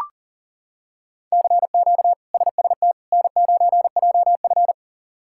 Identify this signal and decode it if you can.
Morse code, heard as a keyed tone, about 40 wpm, 700 Hz. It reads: CQ SST N9JF
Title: K1USN SST QSO Practice - 40wpm